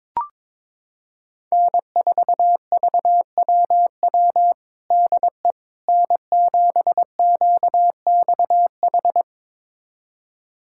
Morse code, heard as a keyed tone, about 22 wpm, 700 Hz. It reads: N4VWW DE N7QXH
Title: USA Callsign DE Callsign - 5 Characters - 22wpm